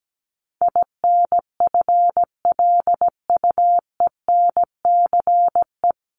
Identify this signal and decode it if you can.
Morse code, heard as a keyed tone, about 17 wpm, 700 Hz. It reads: INFLUENCE
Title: Top 1001-1250 Words 17wpm